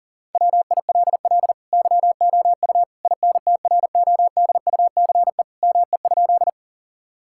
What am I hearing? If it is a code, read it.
Morse code, heard as a keyed tone, about 40 wpm, 700 Hz. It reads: WILL YOU INTRODUCE ME?
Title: Sentences from Top 1600 Words 40wpm